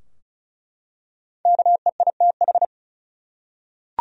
Morse code, heard as a keyed tone, about 36 wpm, 700 Hz. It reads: KEITH
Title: USA Common Names - Max 5 Letters - 35wpm